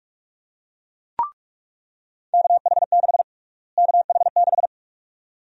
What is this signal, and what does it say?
Morse code, heard as a keyed tone, about 42 wpm, 700 Hz. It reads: KH6 KH6
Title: USA Call Sign Prefixes - Repeated 2x - 45wpm